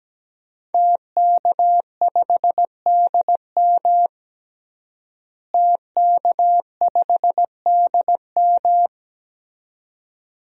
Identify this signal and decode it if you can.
Morse code, heard as a keyed tone, about 17 wpm, 700 Hz. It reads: TK5DM TK5DM
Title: DXCC-100 Call Signs - 5 Characters - Repeated 2x - 17wpm